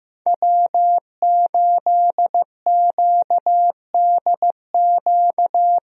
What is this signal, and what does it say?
Morse code, heard as a keyed tone, about 15 wpm, 700 Hz. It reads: W8QDQ
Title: USA Call Signs - 5 Characters - Repeated 2x - 15wpm